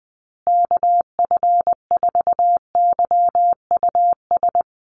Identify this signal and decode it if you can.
Morse code, heard as a keyed tone, about 20 wpm, 700 Hz. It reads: KF4YUS
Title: USA Call Signs Repeated 3x 20wpm